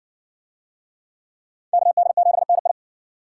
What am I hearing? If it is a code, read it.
Morse code, heard as a keyed tone, about 56 wpm, 700 Hz. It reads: KD7NI